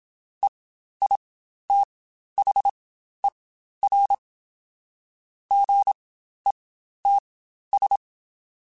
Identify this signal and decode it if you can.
Morse code, heard as a keyed tone, about 27 wpm, spaced as for 12 wpm, 800 Hz. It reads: EITHER GETS
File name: Spaced / Chuckles Spaced-mp3s / Squeaky sp @25